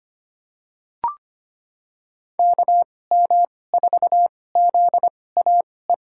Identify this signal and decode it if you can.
Morse code, heard as a keyed tone, about 25 wpm, spaced as for 18 wpm, 700 Hz. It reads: KM4ZAE
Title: USA Call Signs - 2x Character Spacing - 25wpm